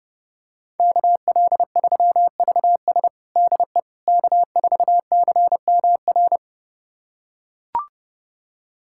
Morse code, heard as a keyed tone, about 30 wpm, 700 Hz. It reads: KL3VS DE K4CMR
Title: USA Callsign DE Callsign - 5 Characters - 30wpm